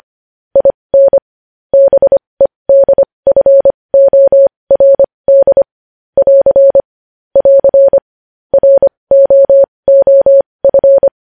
Morse code, heard as a keyed tone, about 25 wpm, spaced as for 20 wpm, 550 Hz. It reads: IN BEDFORD <AR> <AR> ROOF
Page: news headlines in morse code at 20 wpm